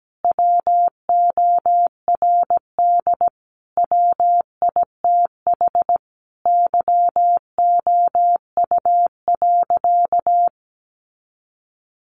Morse code, heard as a keyed tone, about 17 wpm, 700 Hz. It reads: WORD WITH YOU.
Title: Sentences from Top 300 Words 17wpm